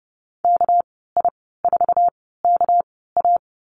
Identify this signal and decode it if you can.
Morse code, heard as a keyed tone, about 32 wpm, spaced as for 17 wpm, 700 Hz. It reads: KI4KA
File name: Lesson 37g - Letter x - Call Signs 30wpm